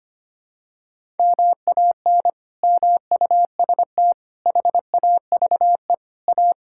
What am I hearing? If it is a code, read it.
Morse code, heard as a keyed tone, about 25 wpm, 700 Hz. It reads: MAN MUST HAVE A
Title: Sentences from Top 1000 Words - Review Part 6 - 25wpm